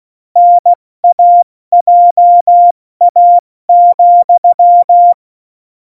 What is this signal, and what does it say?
Morse code, heard as a keyed tone, about 16 wpm, 700 Hz. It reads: NAJA,
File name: Morsekurs / Witze / Witz 02. Juli 2019, 70 F80